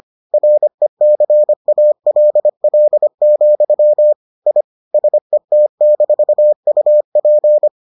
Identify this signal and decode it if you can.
Morse code, heard as a keyed tone, about 25 wpm, 600 Hz. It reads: RECALL, I SET-UP